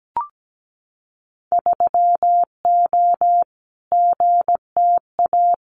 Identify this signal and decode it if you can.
Morse code, heard as a keyed tone, about 17 wpm, 700 Hz. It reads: 3O GTA
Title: Winter Field Day Exchanges Encore 17wpm